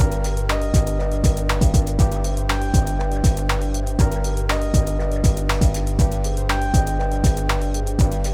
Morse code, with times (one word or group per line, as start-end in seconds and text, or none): none